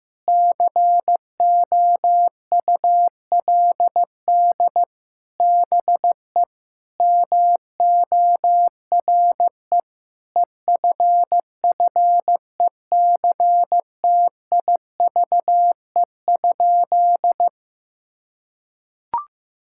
Morse code, heard as a keyed tone, about 15 wpm, 700 Hz. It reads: COULD BE MORE EFFECTIVE?
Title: Sentences from Top 1000 Words - Review Part 6 - 15wpm